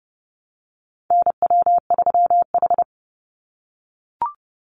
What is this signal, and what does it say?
Morse code, heard as a keyed tone, about 30 wpm, 700 Hz. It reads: NW3H
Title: USA Call Signs 30wpm